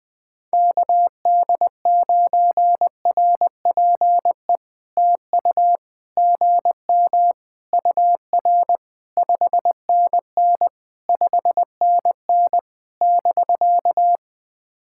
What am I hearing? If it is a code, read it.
Morse code, heard as a keyed tone, about 20 wpm, 700 Hz. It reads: KD9RPE TU GM UR 5NN 5NN <BK>
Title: POTA QSO Practice - 20wpm